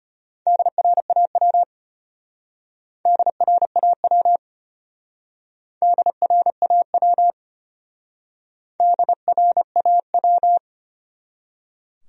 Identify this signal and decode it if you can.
Morse code, heard as a keyed tone, about 33 wpm, 700 Hz. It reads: DRAW DRAW DRAW DRAW